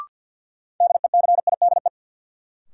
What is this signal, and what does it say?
Morse code, heard as a keyed tone, about 53 wpm, 700 Hz. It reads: DECIDE
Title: Sentences from Top 500 Words - New Word Drill - 50wpm